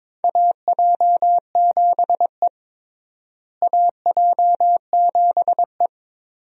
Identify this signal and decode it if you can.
Morse code, heard as a keyed tone, about 22 wpm, 700 Hz. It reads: AJ7E AJ7E
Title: USA Call Signs - 4 Characters - Repeated 2x - 22wpm